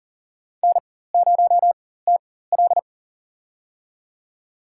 Morse code, heard as a keyed tone, about 37 wpm, spaced as for 18 wpm, 700 Hz. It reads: N0TL
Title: USA Call Signs - 4x Character Spacing - 40wpm